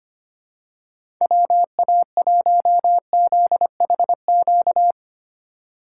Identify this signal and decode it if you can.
Morse code, heard as a keyed tone, about 25 wpm, 700 Hz. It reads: WA1ZHQ
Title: USA Call Signs - 6 Characters - 3x Character Spacing - 25wpm